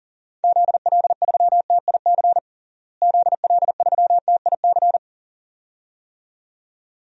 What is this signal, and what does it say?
Morse code, heard as a keyed tone, about 40 wpm, 700 Hz. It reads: ZL3TIC ZL3TIC
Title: DXCC-100 Call Signs - 6 Characters - Repeated 2x - 40wpm